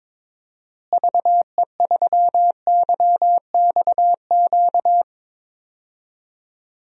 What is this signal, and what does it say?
Morse code, heard as a keyed tone, about 22 wpm, 700 Hz. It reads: VE3YXQ
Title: DXCC-100 Call Signs - 6 Characters - Repeated 3x - 22wpm